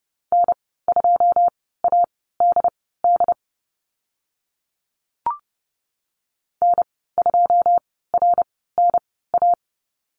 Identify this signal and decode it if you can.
Morse code, heard as a keyed tone, about 30 wpm, spaced as for 17 wpm, 700 Hz. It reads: N2ADD N2RNA
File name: Lesson 14g - Number 2 - Call Signs 30wpm